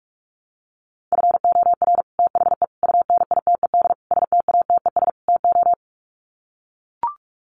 Morse code, heard as a keyed tone, about 44 wpm, 700 Hz. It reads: FOR THE UNITED STATES TO
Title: Sets of 5 Words Encore 45wpm